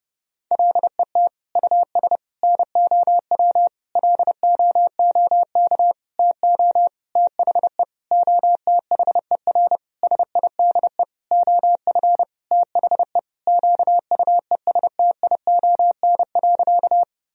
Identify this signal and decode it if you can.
Morse code, heard as a keyed tone, about 32 wpm, 700 Hz. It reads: LET US NOW LOOK TO THE OTHER SIDE OF THE QUESTION.
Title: Sentences from Top 1000 Words - Review Part 1 - 30wpm